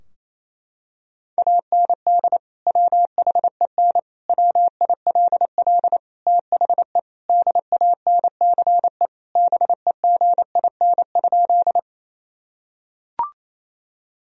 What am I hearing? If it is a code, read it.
Morse code, heard as a keyed tone, about 27 wpm, 700 Hz. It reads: AND WHEN WILL THE DANCE BEGIN?
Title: Sentences from Top 1000 Words - Review Part 4 - 28wpm